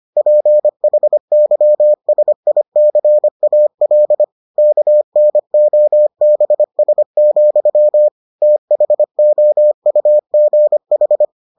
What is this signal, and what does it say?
Morse code, heard as a keyed tone, about 25 wpm, 600 Hz. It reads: PHYSICAL KNOBS, THOUGH